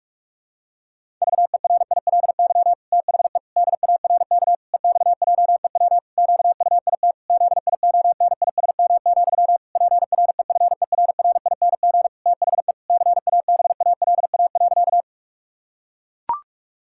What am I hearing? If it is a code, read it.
Morse code, heard as a keyed tone, about 44 wpm, 700 Hz. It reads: VERILY THE DARK EXJEW QUIT ZIONISM, PREFERRING THE CABALA.
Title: Pangram Challenge v2 - 45wpm